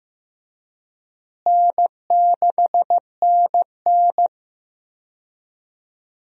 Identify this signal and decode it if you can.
Morse code, heard as a keyed tone, about 15 wpm, 700 Hz. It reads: N6NN
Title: USA Call Signs - 4 Characters - Repeated 2x - 15wpm